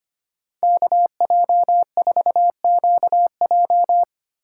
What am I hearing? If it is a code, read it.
Morse code, heard as a keyed tone, about 25 wpm, 700 Hz. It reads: KJ4QJ
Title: USA Call Signs - 5 Characters - 2x Character Spacing - 25wpm